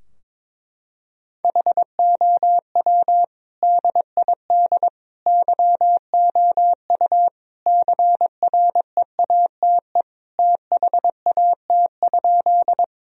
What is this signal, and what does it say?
Morse code, heard as a keyed tone, about 22 wpm, 700 Hz. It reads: HOW DID YOU CREATE THAT?
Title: Sentences from Top 1000 Words - Review Part 3 - 22wpm